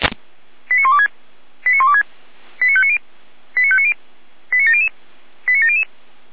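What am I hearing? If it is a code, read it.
Five-tone selective call (ZVEI), heard as quick runs of five tones, tones of about 70 ms. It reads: 89126 89126 89590 89590 89700 89700
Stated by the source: ZVEI five-tone call 89115 received by SDR.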